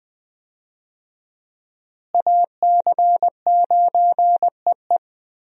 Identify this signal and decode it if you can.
Morse code, heard as a keyed tone, about 20 wpm, 700 Hz. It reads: AC9EE
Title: USA Call Signs - 5 Characters - Repeated 3x - 20wpm